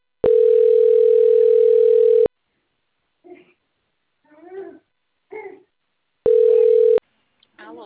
Ringback tone (North America), 440 + 480 Hz.